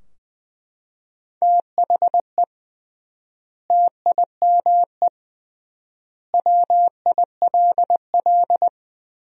Morse code, heard as a keyed tone, about 21 wpm, 700 Hz. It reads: THE TIME WILL
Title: 3x Word-Spacing - Sentences from Top 100 Words - 20wpm